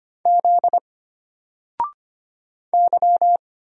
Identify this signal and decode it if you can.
Morse code, heard as a keyed tone, about 25 wpm, 700 Hz. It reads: Z Y
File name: Single-Letter-Number-25wpm